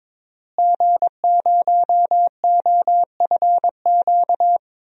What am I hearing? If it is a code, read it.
Morse code, heard as a keyed tone, about 22 wpm, 700 Hz. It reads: G0OFQ
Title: DXCC-100 Call Signs - 5 Characters - Repeated 2x - 22wpm